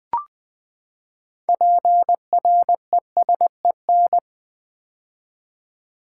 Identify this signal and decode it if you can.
Morse code, heard as a keyed tone, about 20 wpm, 700 Hz. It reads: PRESEN
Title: Build A Word - 6 Letters At A Time - 12-Letter Words - 20wpm